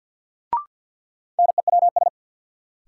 Morse code, heard as a keyed tone, about 53 wpm, 700 Hz. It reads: NEWS
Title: Sets of 5 Words - New Word Drill - 50wpm